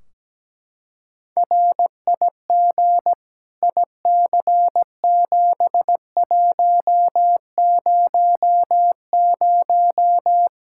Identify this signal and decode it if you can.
Morse code, heard as a keyed tone, about 17 wpm, 700 Hz. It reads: RIG IC7100
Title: Practice QSO - Exchange 2 - Info - 17wpm